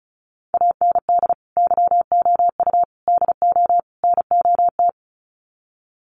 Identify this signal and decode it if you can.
Morse code, heard as a keyed tone, about 34 wpm, 700 Hz. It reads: AND YOU DO NOT
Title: Sets of 4 Words 35wpm